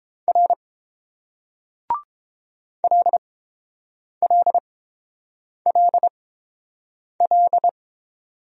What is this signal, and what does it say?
Morse code, heard as a keyed tone, about 29 wpm, spaced as for 7 wpm, 700 Hz. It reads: R LLLL